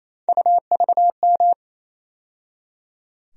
Morse code, heard as a keyed tone, about 28 wpm, 700 Hz. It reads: UVM
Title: USA Call Sign Postfixes - 3 Letters 28wpm